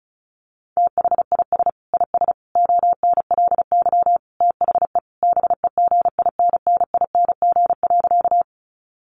Morse code, heard as a keyed tone, about 34 wpm, 700 Hz. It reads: THIS IS ONLY THE BEGINNING.
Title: Sentences from Top 800 Words 35wpm